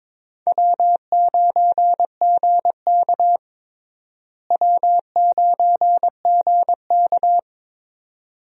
Morse code, heard as a keyed tone, about 22 wpm, 700 Hz. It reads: W9GK W9GK
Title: USA Call Signs - 4 Characters - Repeated 3x - 22wpm